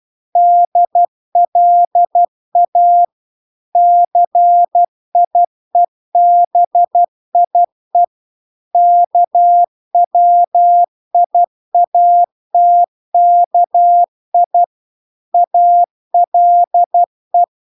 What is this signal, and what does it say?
Morse code, heard as a keyed tone, about 12 wpm, 700 Hz. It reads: DLA CIEBIE KWIATKI ALE